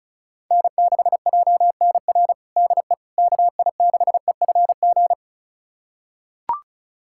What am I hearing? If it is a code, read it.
Morse code, heard as a keyed tone, about 36 wpm, 700 Hz. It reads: N6JNR DE KI6EFG
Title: USA Callsign DE Callsign 35wpm